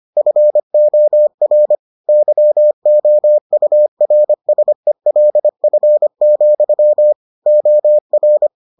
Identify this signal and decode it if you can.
Morse code, heard as a keyed tone, about 25 wpm, 600 Hz. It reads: FOR YOURSELF, OR